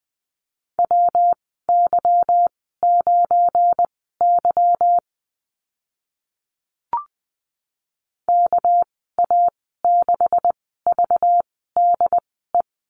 Morse code, heard as a keyed tone, about 20 wpm, spaced as for 15 wpm, 700 Hz. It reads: WY9Y KA6VDE
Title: 238 Lesson 38g - Letter q - Call Signs 20wpm